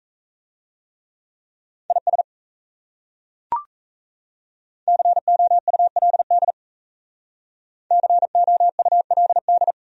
Morse code, heard as a keyed tone, about 37 wpm, 700 Hz. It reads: IS COULD COULD